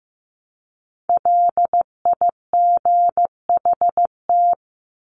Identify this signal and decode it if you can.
Morse code, heard as a keyed tone, about 15 wpm, 700 Hz. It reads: LIGHT
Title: Top 300 Words 15wpm